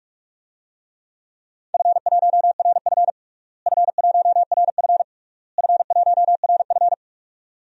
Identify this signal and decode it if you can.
Morse code, heard as a keyed tone, about 41 wpm, 700 Hz. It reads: F1RF F1RF F1RF
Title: DXCC-100 Call Signs - 4 Characters - Repeated 3x - 45wpm